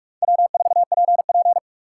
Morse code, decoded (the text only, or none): W4PP